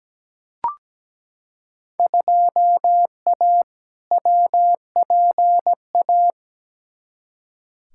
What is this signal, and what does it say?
Morse code, heard as a keyed tone, about 17 wpm, 700 Hz. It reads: 2A WPA
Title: Field Day Exchanges - 17wpm